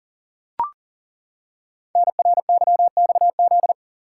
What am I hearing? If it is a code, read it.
Morse code, heard as a keyed tone, about 39 wpm, 700 Hz. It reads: NRYXZ
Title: Code Groups - 5 Letters - 40wpm